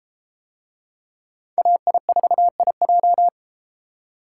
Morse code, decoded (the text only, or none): AI4IJ